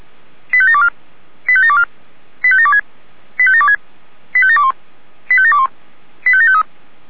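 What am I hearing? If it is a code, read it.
Five-tone selective call (ZVEI), heard as quick runs of five tones, tones of about 70 ms. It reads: 85724 85724 85726 85726 85721 85721 85753
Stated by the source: ZVEI selective call received by SDR.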